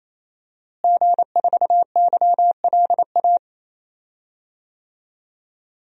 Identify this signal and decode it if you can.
Morse code, heard as a keyed tone, about 28 wpm, 700 Hz. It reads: G4YLA
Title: DXCC-100 Call Signs - 5 Characters - Repeated 3x - 28wpm